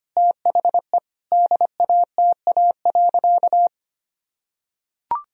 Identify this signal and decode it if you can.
Morse code, heard as a keyed tone, about 25 wpm, 700 Hz. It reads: THE DATA.
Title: Sentences from Top 1000 Words - Review Part 3 - 25wpm